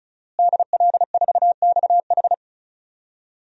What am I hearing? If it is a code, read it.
Morse code, heard as a keyed tone, about 34 wpm, 700 Hz. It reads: DL4XH
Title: DXCC-100 Call Signs - 5 Characters - Repeated 3x - 35wpm